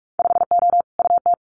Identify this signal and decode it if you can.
Morse code, heard as a keyed tone, about 46 wpm, 700 Hz. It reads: 5O UT
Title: Winter Field Day Exchanges 45wpm